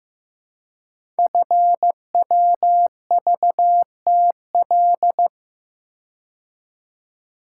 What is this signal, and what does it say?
Morse code, heard as a keyed tone, about 15 wpm, 700 Hz. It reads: FWVTL
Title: Code Groups - 5 Letter:Numbers - 15wpm